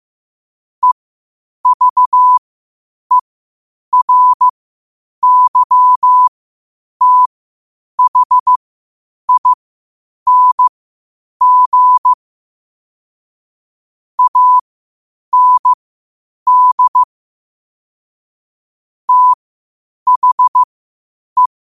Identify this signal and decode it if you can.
Morse code, heard as a keyed tone, about 15 wpm, spaced as for 8 wpm, 1000 Hz. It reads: EVERYTHING AND THE
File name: qotdfiles / 8wpm